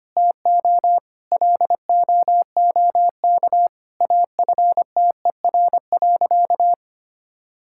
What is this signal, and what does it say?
Morse code, heard as a keyed tone, about 25 wpm, 700 Hz. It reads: TO LOOK AFTER.
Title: Sentences from Top 1600 Words 25wpm